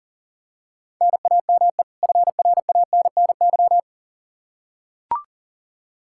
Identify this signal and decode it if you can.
Morse code, heard as a keyed tone, about 37 wpm, 700 Hz. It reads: NAME FRANNY
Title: Practice QSO - Exchange 1 - Beginning - 40wpm